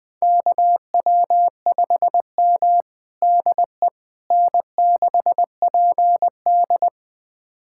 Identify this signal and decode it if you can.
Morse code, heard as a keyed tone, about 20 wpm, 700 Hz. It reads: KW5M DE N6PD
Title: USA Callsign DE Callsign - 4 Characters - 20wpm